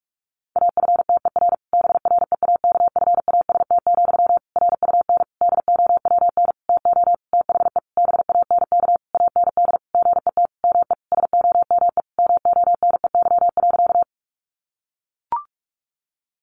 Morse code, heard as a keyed tone, about 44 wpm, 700 Hz. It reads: AFTER BREAKFAST, RUN DOWN TO THE BANK AND GET ME SOME MONEY.
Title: Sentences from Top 1500 Words 45wpm